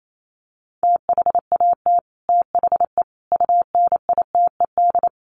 Morse code, heard as a keyed tone, about 28 wpm, 700 Hz. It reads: THAT THE UNITED
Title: Sets of 4 Words 28wpm